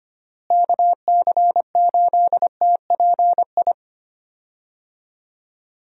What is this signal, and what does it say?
Morse code, heard as a keyed tone, about 25 wpm, 700 Hz. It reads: KC8TPI
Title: USA Call Signs - 6 Characters - Repeated 2x - 25wpm